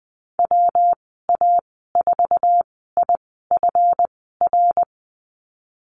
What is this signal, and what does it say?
Morse code, heard as a keyed tone, about 20 wpm, spaced as for 15 wpm, 700 Hz. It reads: WA4IFR
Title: 186 Lesson 30g - Letter b - Call Signs 20wpm